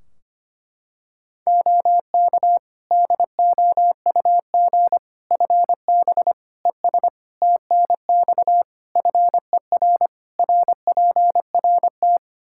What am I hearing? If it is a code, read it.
Morse code, heard as a keyed tone, about 25 wpm, 700 Hz. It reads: OK DOUG FB ES TNX FER RPRT
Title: K1USN SST QSO Practice 25wpm